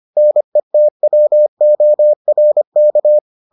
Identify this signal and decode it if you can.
Morse code, heard as a keyed tone, about 25 wpm, 600 Hz. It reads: NETWORK